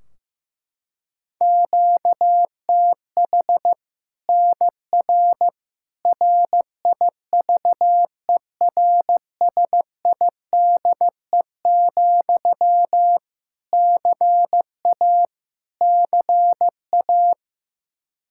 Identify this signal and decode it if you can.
Morse code, heard as a keyed tone, about 16 wpm, 700 Hz. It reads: QTH NR RIVERSIDE, CA CA
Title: Practice QSO - Exchange 1 - Beginning - 15wpm